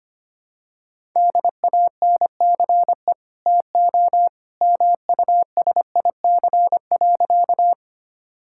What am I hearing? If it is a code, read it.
Morse code, heard as a keyed tone, about 25 wpm, 700 Hz. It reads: DANCE TO MUSIC.
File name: Easy Sentences with Hints - 25wpm